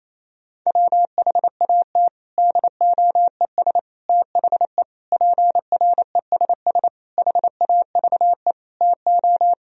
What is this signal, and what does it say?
Morse code, heard as a keyed tone, about 28 wpm, 700 Hz. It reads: WHAT DOES THE PRESS HAVE TO
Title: Sentences from Top 1000 Words - Review Part 3 - 28wpm